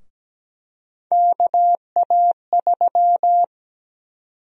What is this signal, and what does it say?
Morse code, heard as a keyed tone, about 17 wpm, 700 Hz. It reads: KA3
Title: USA Call Sign Prefixes - Repeated 2x - 17wpm